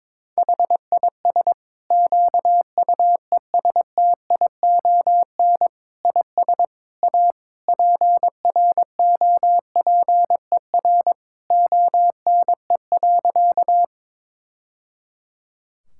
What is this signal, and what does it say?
Morse code, heard as a keyed tone, about 22 wpm, 700 Hz. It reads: HIS QUESTION IS A PROPER ONE.
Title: Sentences from Top 1000 Words - Review Part 5 - 22wpm